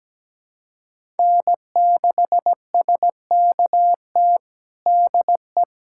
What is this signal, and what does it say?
Morse code, heard as a keyed tone, about 17 wpm, 700 Hz. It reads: N6SKT DE
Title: USA Callsign DE Callsign - 5 Characters - 17wpm